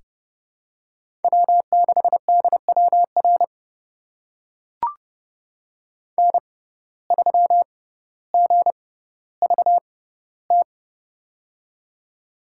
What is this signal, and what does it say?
Morse code, heard as a keyed tone, about 30 wpm, spaced as for 15 wpm, 700 Hz. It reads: W6DWR N3GVT
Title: USA Call Signs - 5 Characters - 6x Character Spacing - 30wpm